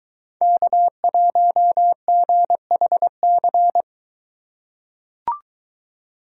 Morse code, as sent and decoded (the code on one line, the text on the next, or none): -.- .---- --. .... -.-.
K1GHC